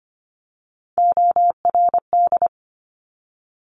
Morse code, heard as a keyed tone, about 25 wpm, 700 Hz. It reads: ORD
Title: Top Three Letter Combinations in Words 25wpm